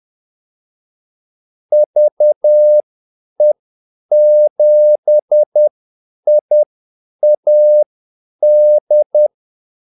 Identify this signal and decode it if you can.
Morse code, heard as a keyed tone, about 10 wpm, spaced as for 8 wpm, 600 Hz. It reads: VE7IAD
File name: Callsigns-10-8wpm